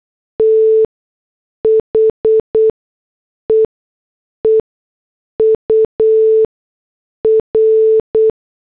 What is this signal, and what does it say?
Morse code, heard as a keyed tone, about 8 wpm, spaced as for 6 wpm, 440 Hz. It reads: THE EUR